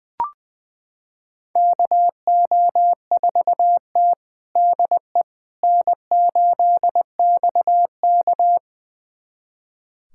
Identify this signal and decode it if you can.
Morse code, heard as a keyed tone, about 19 wpm, 700 Hz. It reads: KO4T DE N8XK
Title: USA Callsign DE Callsign - 4 Characters - 20wpm